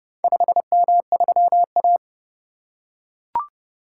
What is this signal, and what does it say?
Morse code, heard as a keyed tone, about 29 wpm, 700 Hz. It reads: 5M3A